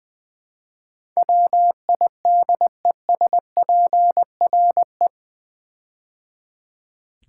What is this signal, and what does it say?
Morse code, heard as a keyed tone, about 20 wpm, 700 Hz. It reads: WIDESPRE
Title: Build A Word - 4 Letters At A Time - 10-Letter Words - 20wpm